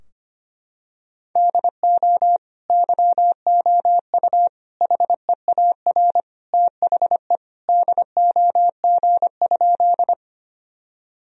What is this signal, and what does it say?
Morse code, heard as a keyed tone, about 25 wpm, 700 Hz. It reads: DO YOU HEAR THE DOG?
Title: Sentences from Top 1000 Words - Review Part 2 - 25wpm